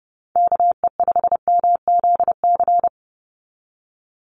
Morse code, heard as a keyed tone, about 30 wpm, 700 Hz. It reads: KE5MZC
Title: USA Call Signs 30wpm